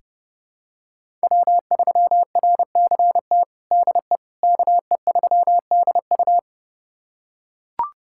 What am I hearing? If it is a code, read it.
Morse code, heard as a keyed tone, about 31 wpm, 700 Hz. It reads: W3RCT DE KE3DU
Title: USA Callsign DE Callsign - 5 Characters - 30wpm